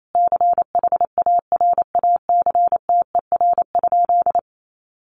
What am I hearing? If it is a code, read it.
Morse code, heard as a keyed tone, about 28 wpm, 700 Hz. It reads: CHARACTER?
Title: Sentences from Top 1200 Words 28wpm